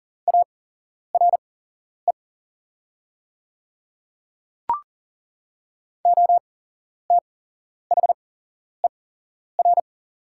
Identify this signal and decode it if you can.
Morse code, heard as a keyed tone, about 40 wpm, spaced as for 11 wpm, 690 Hz. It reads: ARE OTHER
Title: 8x Character Spacing - Top 100 Words - 40wpm